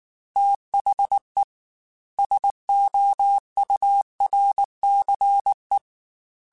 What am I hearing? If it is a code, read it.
Morse code, heard as a keyed tone, about 19 wpm, 800 Hz. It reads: THE SOURCE
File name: Technical Subjects / Technical Subjects-mp3s / XC @18